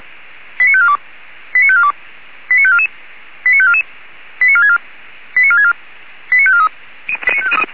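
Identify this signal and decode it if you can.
Five-tone selective call (ZVEI), heard as quick runs of five tones, tones of about 70 ms. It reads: 89542 89542 89544 89544 89464 89464 89543
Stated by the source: ZVEI five-tone selective call received with an SDR.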